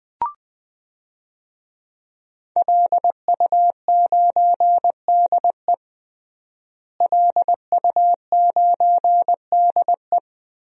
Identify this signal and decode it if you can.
Morse code, heard as a keyed tone, about 21 wpm, 700 Hz. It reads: LU9DE LU9DE
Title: DXCC-100 Call Signs - 5 Characters - Repeated 3x - 20wpm